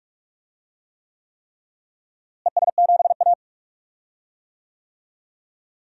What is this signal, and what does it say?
Morse code, heard as a keyed tone, about 46 wpm, 700 Hz. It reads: ES7A
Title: DXCC-100 Call Signs - 4 Characters - Repeated 3x - 45wpm